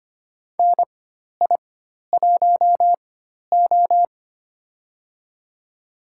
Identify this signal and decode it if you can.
Morse code, heard as a keyed tone, about 25 wpm, spaced as for 12 wpm, 700 Hz. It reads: NI1O
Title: USA Call Signs - 4 Characters - 4x Character Spacing - 25wpm